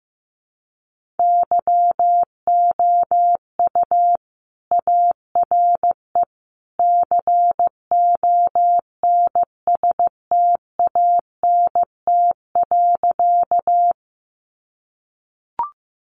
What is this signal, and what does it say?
Morse code, heard as a keyed tone, about 15 wpm, 700 Hz. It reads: YOU ARE CONSTANT.
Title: Sentences from Top 1500 Words 15wpm